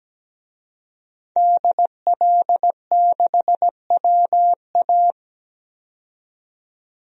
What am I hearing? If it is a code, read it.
Morse code, heard as a keyed tone, about 17 wpm, 700 Hz. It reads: DL6WA
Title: DXCC-100 Call Signs - 5 Characters - 17wpm